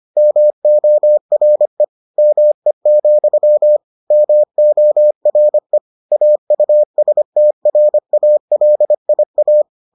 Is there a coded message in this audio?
Yes: Morse code from 0.2 to 9.6 s, heard as a keyed tone, about 25 wpm, 600 Hz. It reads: MORE ME, MORE AUSTRALIA